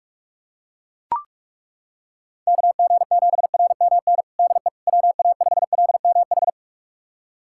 Morse code, heard as a keyed tone, about 41 wpm, 700 Hz. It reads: KG7RMN DE WA5LMH
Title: USA Callsign DE Callsign - 6 Characters - 45wpm